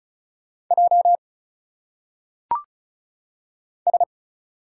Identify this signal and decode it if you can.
Morse code, heard as a keyed tone, about 36 wpm, 700 Hz. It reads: J S
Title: Single Letters - Phonetic Pronunciation 35wpm